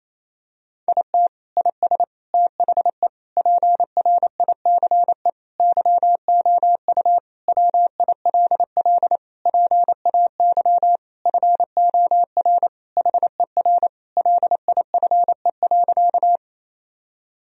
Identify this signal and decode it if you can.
Morse code, heard as a keyed tone, about 27 wpm, 700 Hz. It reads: IT IS THE PRICE YOU WILL PAY FOR HER LIFE.
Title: Sentences from Top 1000 Words - Review Part 4 - 28wpm